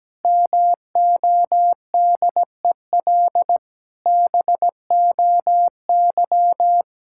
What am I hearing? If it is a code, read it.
Morse code, heard as a keyed tone, about 17 wpm, 700 Hz. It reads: MODEL BOY
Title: Sentences from Top 1000 Words - Review Part 5 - 17wpm